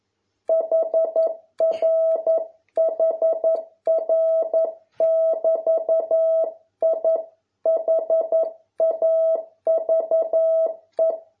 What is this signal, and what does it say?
Morse code, heard as a keyed tone, about 11 wpm, 650 Hz. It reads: HR HR <BT> I HAVE